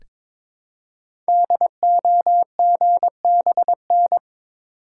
Morse code, heard as a keyed tone, about 22 wpm, 700 Hz. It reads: DOGBN
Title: Code Groups - 5 Letters - 22wpm